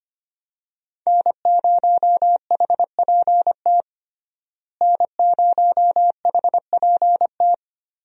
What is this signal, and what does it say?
Morse code, heard as a keyed tone, about 25 wpm, 700 Hz. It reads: N0HPT N0HPT
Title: USA Call Signs - 5 Characters - Repeated 3x - 25wpm